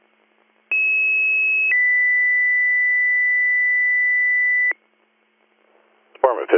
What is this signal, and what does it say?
Two-tone page, heard as a one-second tone, then a three-second tone, about 2600 then 2000 Hz, with speech. It is heard from 0.7 to 4.7 s.